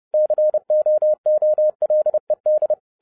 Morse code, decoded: COOLED

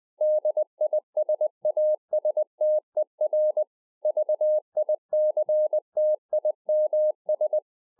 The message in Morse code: DISASTER VICTIMS

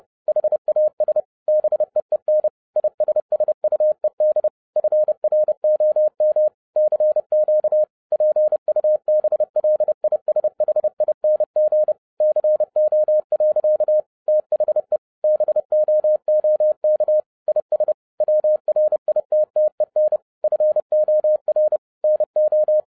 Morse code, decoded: HAS BEEN ISSUED FROM CQ PUBLISHING CO. THE BOOK IS WRITTEN FOR NO

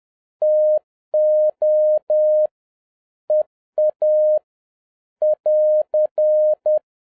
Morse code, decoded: TO EA <AR>

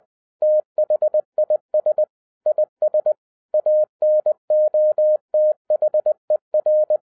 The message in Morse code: THIS IS ANOTHER

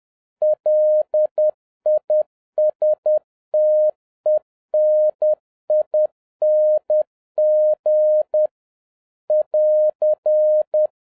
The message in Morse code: LISTENING <AR>

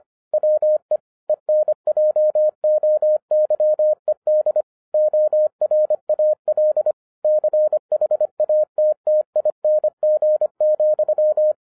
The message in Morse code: WE ENJOYED ORAL CHATTING,

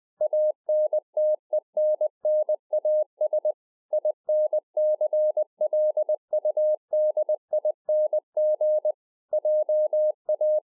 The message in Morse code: ANTENNAS INCLUDING JA

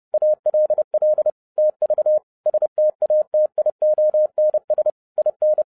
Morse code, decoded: ALL TV STATIONS IN

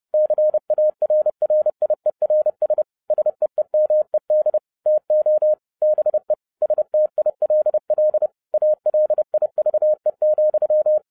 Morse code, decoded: CARRIERS SEEMED TO BE STILL ALIVE,

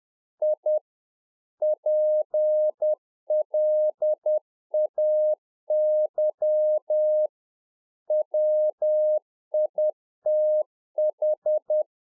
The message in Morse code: I PLAY WITH

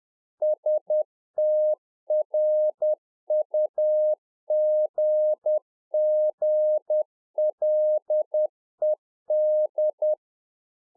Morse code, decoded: STRUGGLED

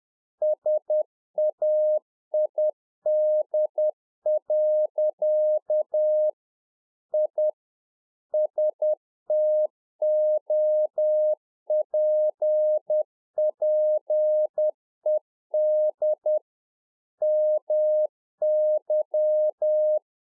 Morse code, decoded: SAID. I STOPPED MY

